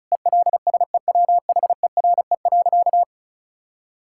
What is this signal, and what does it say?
Morse code, heard as a keyed tone, about 34 wpm, 700 Hz. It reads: ELSEWHERE.
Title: Sentences from Top 1600 Words 35wpm